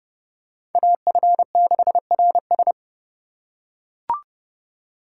Morse code, heard as a keyed tone, about 32 wpm, 700 Hz. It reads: AF6RS